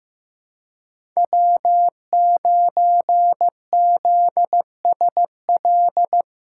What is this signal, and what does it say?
Morse code, heard as a keyed tone, about 15 wpm, 700 Hz. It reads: W9ZSL
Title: USA Callsign DE Callsign - 5 Characters - 15wpm